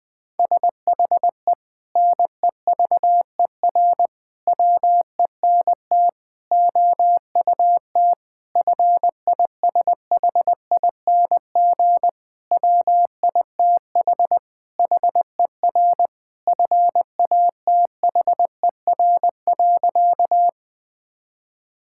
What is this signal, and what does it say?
Morse code, heard as a keyed tone, about 20 wpm, 700 Hz. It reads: SHE NEVER WENT OUT FISHING WITH HER FATHER.
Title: Sentences from Top 1000 Words - Review Part 5 - 20wpm